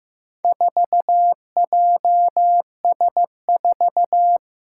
Morse code, heard as a keyed tone, about 15 wpm, 700 Hz. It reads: 4JS4
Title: Four Challenging Letter-Number Combinations 15wpm